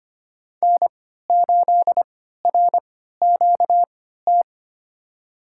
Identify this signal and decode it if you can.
Morse code, heard as a keyed tone, about 25 wpm, spaced as for 14 wpm, 700 Hz. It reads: N8RQT
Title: USA Call Signs - 5 Characters - 3x Character Spacing - 25wpm